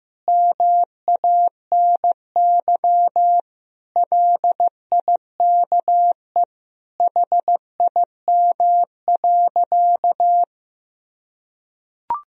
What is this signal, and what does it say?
Morse code, heard as a keyed tone, about 15 wpm, 700 Hz. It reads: MANY LIKE HIM.
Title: Sentences from Top 1000 Words - Review Part 1 - 15wpm